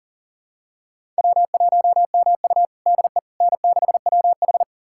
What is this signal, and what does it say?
Morse code, heard as a keyed tone, about 40 wpm, 700 Hz. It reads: W1MU DE N6WH
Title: USA Callsign DE Callsign - 4 Characters - 40wpm